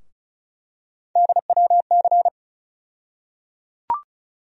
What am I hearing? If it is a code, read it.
Morse code, heard as a keyed tone, about 34 wpm, 700 Hz. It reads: DWC
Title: USA Call Sign Postfixes - 3 Letters 35wpm